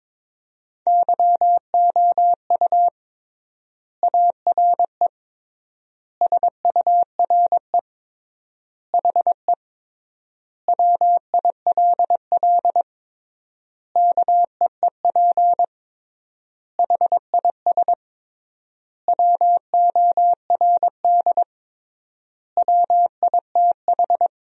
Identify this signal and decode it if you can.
Morse code, heard as a keyed tone, about 22 wpm, 700 Hz. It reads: YOU ARE SURE HE WILL KEEP HIS WORD WITH
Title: 3x Word-Spacing - Sentences from Top 300 Words - 22wpm